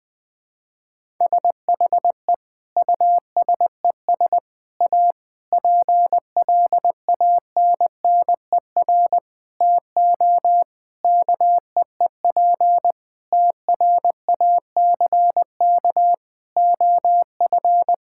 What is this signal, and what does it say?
Morse code, heard as a keyed tone, about 21 wpm, 700 Hz. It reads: SHE USES A PLANNER TO KEEP TRACK OF
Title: Difficult Sentences with Hints - 20wpm